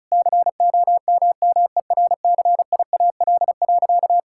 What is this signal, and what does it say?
Morse code, heard as a keyed tone, about 36 wpm, 700 Hz. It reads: COMMERCIAL.
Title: Sentences from Top 1000 Words - Review Part 5 - 35wpm